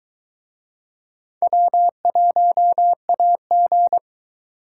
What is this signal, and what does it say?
Morse code, heard as a keyed tone, about 23 wpm, 700 Hz. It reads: W1AG